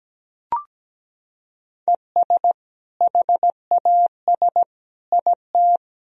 Morse code, heard as a keyed tone, about 17 wpm, 700 Hz. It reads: ES HAS IT
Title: Sentences from Top 100 Words and Abbreviations 17wpm